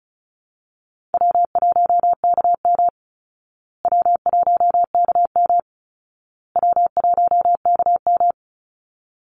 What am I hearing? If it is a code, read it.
Morse code, heard as a keyed tone, about 34 wpm, 700 Hz. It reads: W1KM W1KM W1KM
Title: Easy Call Signs Repeated 3x - 35wpm